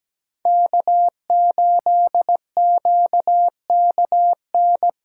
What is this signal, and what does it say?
Morse code, heard as a keyed tone, about 20 wpm, 700 Hz. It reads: K8QKN
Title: Speed Racing - USA Call Signs 17wpm